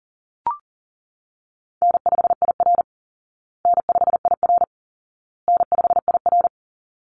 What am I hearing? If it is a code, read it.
Morse code, heard as a keyed tone, about 40 wpm, 700 Hz. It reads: N5IR N5IR N5IR
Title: Call Signs Repeated 2x - 40wpm